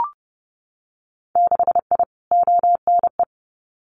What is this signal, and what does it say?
Morse code, heard as a keyed tone, about 30 wpm, 700 Hz. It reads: 6I ONE
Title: Winter Field Day Exchanges Encore 30wpm